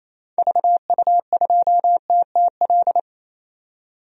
Morse code, heard as a keyed tone, about 28 wpm, 700 Hz. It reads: VU2TTL
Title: DXCC-100 Call Signs - 6 Characters - 28wpm